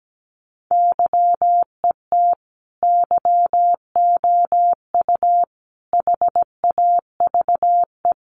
Morse code, heard as a keyed tone, about 17 wpm, 700 Hz. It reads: YET YOU HAVE
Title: Sentences from Top 600 Words 17wpm